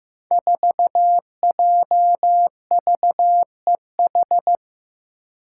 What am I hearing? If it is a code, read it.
Morse code, heard as a keyed tone, about 15 wpm, 700 Hz. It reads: 4JVEH
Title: Five Challenging Letter-Number Combinations 15wpm